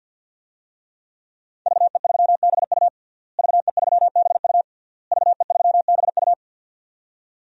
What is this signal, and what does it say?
Morse code, heard as a keyed tone, about 52 wpm, 700 Hz. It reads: VE3BU VE3BU VE3BU
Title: DXCC-100 Call Signs - 5 Characters - Repeated 3x - 50wpm